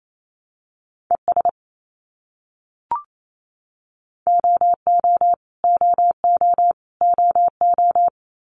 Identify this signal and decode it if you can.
Morse code, heard as a keyed tone, about 28 wpm, 700 Hz. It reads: ES OO OO OO
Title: Abbreviations Repeated 3x - 28wpm